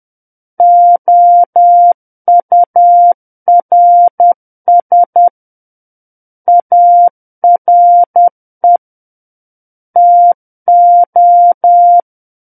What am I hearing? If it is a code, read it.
Morse code, heard as a keyed tone, about 10 wpm, 700 Hz. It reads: OURS ARE TO
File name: Book One / 01-Chapter 1 - 10 WPM